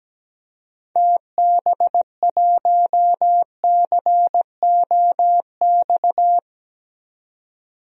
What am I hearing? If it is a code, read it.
Morse code, heard as a keyed tone, about 17 wpm, 700 Hz. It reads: TB1COX
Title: DXCC-100 Call Signs - 6 Characters - 17wpm